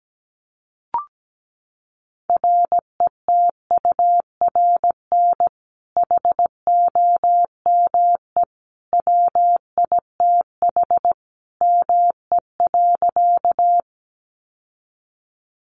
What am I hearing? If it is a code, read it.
Morse code, heard as a keyed tone, about 17 wpm, 700 Hz. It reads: RETURN HOME WITH ME.
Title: Sentences from Top 800 Words 17wpm